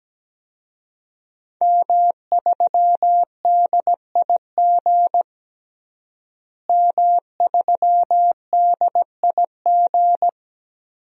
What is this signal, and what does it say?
Morse code, heard as a keyed tone, about 17 wpm, 700 Hz. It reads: M3DIG M3DIG
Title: DXCC-100 Call Signs - 5 Characters - Repeated 3x - 17wpm